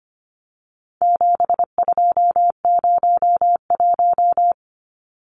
Morse code, heard as a keyed tone, about 25 wpm, 700 Hz. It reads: 7201
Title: Numbers - 4-Digit 25wpm